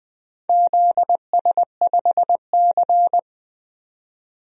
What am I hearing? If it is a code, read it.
Morse code, heard as a keyed tone, about 20 wpm, 700 Hz. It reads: ZS5C
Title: DXCC-100 Call Signs - 4 Characters - 20wpm